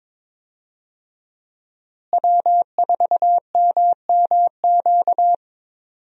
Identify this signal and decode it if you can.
Morse code, heard as a keyed tone, about 22 wpm, 700 Hz. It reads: W4MMQ
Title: USA Call Signs - 5 Characters - Repeated 3x - 22wpm